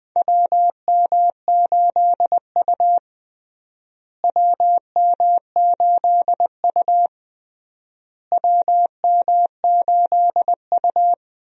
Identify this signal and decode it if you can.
Morse code, heard as a keyed tone, about 21 wpm, 700 Hz. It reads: WM8U WM8U WM8U
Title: USA Call Signs - 4 Characters - Repeated 3x - 20wpm